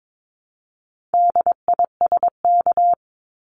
Morse code, heard as a keyed tone, about 22 wpm, 700 Hz. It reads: DISK
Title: Top 751-1000 Words 22wpm